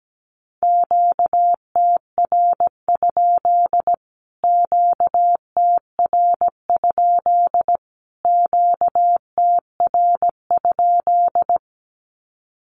Morse code, heard as a keyed tone, about 17 wpm, 700 Hz. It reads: QTR? QTR? QTR?
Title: Q-Codes Repeated 3x 17wpm